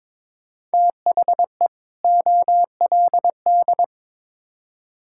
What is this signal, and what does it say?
Morse code, heard as a keyed tone, about 22 wpm, 700 Hz. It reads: THE OLD
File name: Build A Phrase - 2-Word Phrases - 22wpm